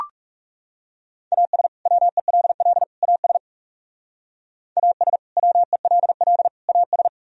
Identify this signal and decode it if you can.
Morse code, heard as a keyed tone, about 38 wpm, 700 Hz. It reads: AS WELL AS AS WELL AS